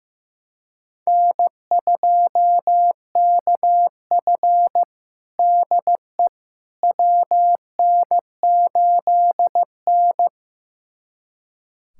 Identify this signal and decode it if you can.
Morse code, heard as a keyed tone, about 15 wpm, 700 Hz. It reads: N2KF DE WN8N
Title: USA Callsign DE Callsign - 4 Characters - 15wpm